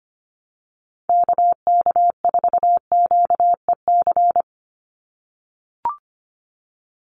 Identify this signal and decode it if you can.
Morse code, heard as a keyed tone, about 25 wpm, 700 Hz. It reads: KK4QEC
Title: USA Call Signs 25wpm